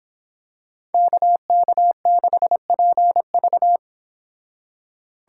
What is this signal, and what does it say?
Morse code, heard as a keyed tone, about 26 wpm, 700 Hz. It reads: KK6PV